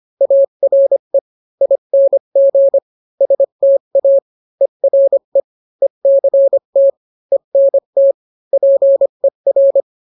Morse code, heard as a keyed tone, about 25 wpm, 550 Hz. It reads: ARE ING STA ERE ECT ENT PER